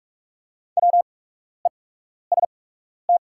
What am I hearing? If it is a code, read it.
Morse code, heard as a keyed tone, about 44 wpm, spaced as for 12 wpm, 690 Hz. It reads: WEST
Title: 8x Character Spacing - Top 500 Words - 45wpm